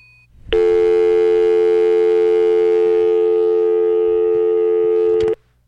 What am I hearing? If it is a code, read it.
Dial tone (North America), 350 + 440 Hz.